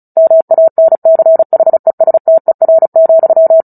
Morse code, heard as a keyed tone, about 34 wpm, 650 Hz. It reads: MANCHESTER,
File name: Book Two / 09-Chapter 9 - 35 WPM